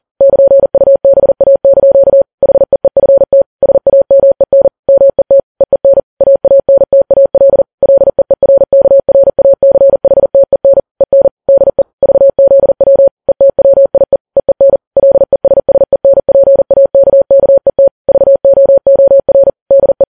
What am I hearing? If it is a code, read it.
Morse code, heard as a keyed tone, about 40 wpm, 550 Hz. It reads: <KN>UBA) HEEFT SAMEN MET EEN AANTAL LEERKRACHTEN EN DE VZW ETWIE EEN LESSENPAKKET VOOR DE
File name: Files Web tls / Morse on9fv / morse40kps